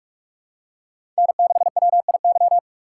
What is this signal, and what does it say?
Morse code, heard as a keyed tone, about 42 wpm, 700 Hz. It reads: N6WIY
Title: USA Call Signs - 5 Characters - 6x Character Spacing - 45wpm